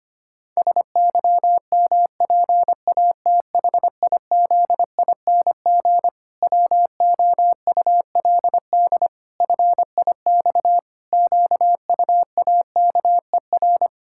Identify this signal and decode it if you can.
Morse code, heard as a keyed tone, about 25 wpm, 700 Hz. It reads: SYMPATHIZING WOULD FIX QUAKER
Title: Pangram Challenge v2 - 25wpm